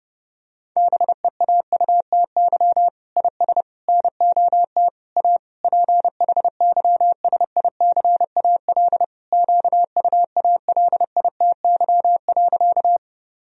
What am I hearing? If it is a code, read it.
Morse code, heard as a keyed tone, about 28 wpm, 700 Hz. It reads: BEAUTY IS NOT A PHYSICAL QUALITY.
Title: Sentences from Top 1000 Words - Review Part 4 - 30wpm